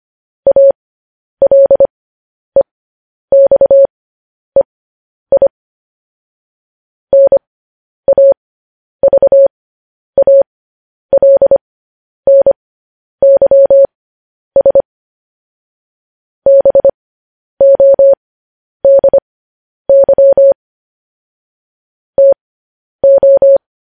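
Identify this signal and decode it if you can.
Morse code, heard as a keyed tone, about 25 wpm, spaced as for 10 wpm, 550 Hz. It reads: ALEXEI NAVALNYS BODY TO
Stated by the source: Morse code news headlines, likely sent at 10 WPM.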